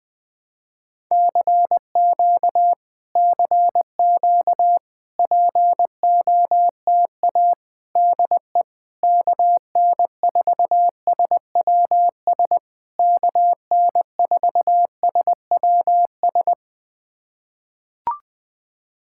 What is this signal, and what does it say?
Morse code, heard as a keyed tone, about 20 wpm, 700 Hz. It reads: CQ CQ POTA DE KN4SWS KN4SWS
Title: POTA QSO Practice - 20wpm